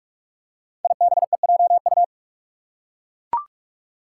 Morse code, heard as a keyed tone, about 44 wpm, 700 Hz. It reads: IBEJU
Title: Five Challenging Letter-Number Combinations 45wpm